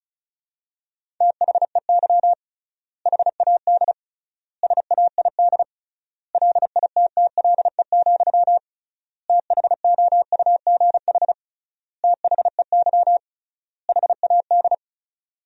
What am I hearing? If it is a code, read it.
Morse code, heard as a keyed tone, about 34 wpm, 700 Hz. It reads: THEY HAD SAID LITTLE, THOUGH THEY HAD
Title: 3x Word-Spacing - Sentences from Top 400 Words - 35wpm